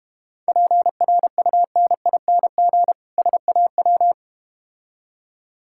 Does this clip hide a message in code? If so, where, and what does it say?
Yes: Morse code from 0.5 to 4.1 s, heard as a keyed tone, about 32 wpm, 700 Hz. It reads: PRUNING SAW